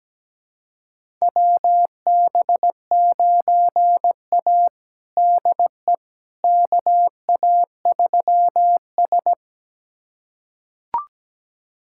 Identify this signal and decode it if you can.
Morse code, heard as a keyed tone, about 17 wpm, 700 Hz. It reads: WB9A DE KA3S
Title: USA Callsign DE Callsign - 4 Characters - 17wpm